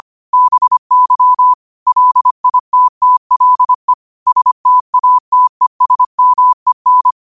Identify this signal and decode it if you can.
Morse code, heard as a keyed tone, about 25 wpm, 1000 Hz. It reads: BY LITTLE STATESMEN
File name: qotdfiles / 25wpm